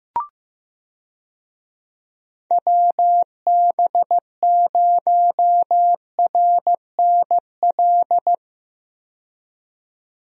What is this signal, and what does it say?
Morse code, heard as a keyed tone, about 15 wpm, 700 Hz. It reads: WB0RNL
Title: USA Call Signs - 6 Characters - 15wpm